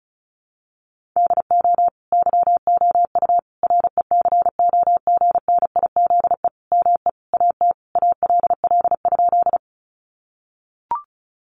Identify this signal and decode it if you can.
Morse code, heard as a keyed tone, about 34 wpm, 700 Hz. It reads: DO YOU RECOGNIZE ME AT ALL?
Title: Sentences from Top 1400 Words 35wpm